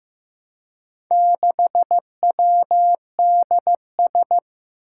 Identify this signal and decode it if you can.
Morse code, heard as a keyed tone, about 15 wpm, 700 Hz. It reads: 6WDS
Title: Four Challenging Letter-Number Combinations 15wpm